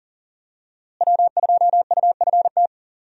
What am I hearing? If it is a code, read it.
Morse code, heard as a keyed tone, about 40 wpm, 700 Hz. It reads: W2UFT
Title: USA Call Signs - 5 Characters - Repeated 3x - 40wpm